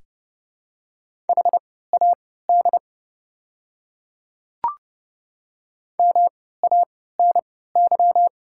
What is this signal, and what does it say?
Morse code, heard as a keyed tone, about 30 wpm, spaced as for 17 wpm, 700 Hz. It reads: HAD MANY
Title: 3x Character Spacing - Top 100 Words - 30wpm